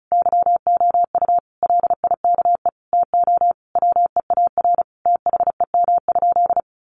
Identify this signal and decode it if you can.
Morse code, heard as a keyed tone, about 34 wpm, 700 Hz. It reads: YOU LIKE TO WEAR THEM?
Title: Sentences from Top 1500 Words 35wpm